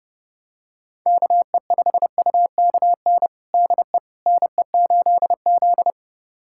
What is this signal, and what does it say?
Morse code, heard as a keyed tone, about 31 wpm, 700 Hz. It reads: KE5UKN DE NE8Z
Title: USA Callsign DE Callsign 30wpm